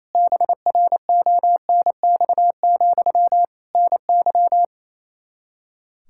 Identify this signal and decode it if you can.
Morse code, heard as a keyed tone, about 28 wpm, 700 Hz. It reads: BRONX, NY
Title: K1USN SST QSO Practice 28wpm